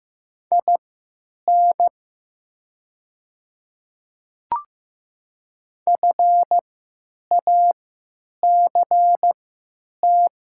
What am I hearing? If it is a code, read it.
Morse code, heard as a keyed tone, about 15 wpm, spaced as for 8 wpm, 700 Hz. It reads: IN FACT
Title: 3x Character Spacing - Top 500 Words - 15wpm